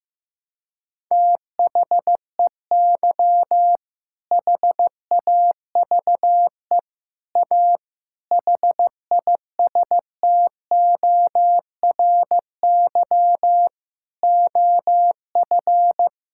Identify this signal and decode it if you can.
Morse code, heard as a keyed tone, about 15 wpm, 700 Hz. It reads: THEY HAVE A HISTORY OF
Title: Sentences from Top 1000 Words - Review Part 6 - 15wpm